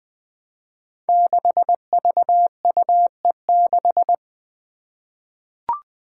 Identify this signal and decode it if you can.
Morse code, heard as a keyed tone, about 20 wpm, 700 Hz. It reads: 6VUE6
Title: Five Challenging Letter-Number Combinations 20wpm